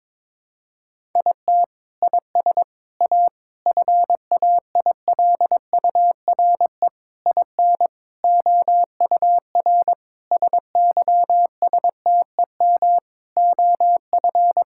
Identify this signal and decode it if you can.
Morse code, heard as a keyed tone, about 22 wpm, 700 Hz. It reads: IT IS A FAILURE IN OUR SYSTEM OF
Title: Sentences from Top 1000 Words - Review Part 6 - 22wpm